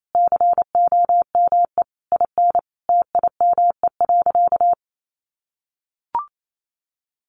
Morse code, heard as a keyed tone, about 28 wpm, 700 Hz. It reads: COME IN TIME.
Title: Sentences from Top 600 Words 28wpm